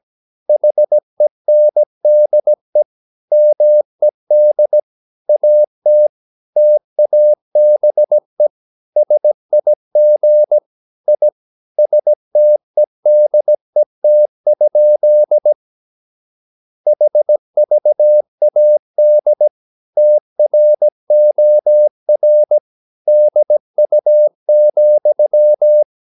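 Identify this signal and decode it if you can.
Morse code, heard as a keyed tone, about 17 wpm, 600 Hz. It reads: HENDE MED AT TABE SIG I STEDET? HVAD TROR DU,
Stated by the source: Slow (QRS) Morse practice at 17 wpm, from CW med Gnister.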